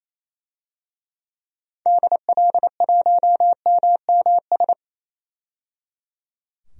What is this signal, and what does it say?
Morse code, heard as a keyed tone, about 28 wpm, 700 Hz. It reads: DL1MMS
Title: DXCC-100 Call Signs - 6 Characters - 28wpm